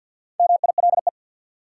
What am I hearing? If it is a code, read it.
Morse code, heard as a keyed tone, about 53 wpm, 700 Hz. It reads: MILE
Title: Sentences from Top 400 Words - New Word Drill - 50wpm